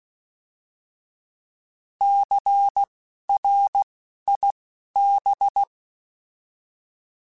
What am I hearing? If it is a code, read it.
Morse code, heard as a keyed tone, about 16 wpm, spaced as for 12 wpm, 800 Hz. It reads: CRIB
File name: Alphabetical / Alphabetical mp3s / Cs @10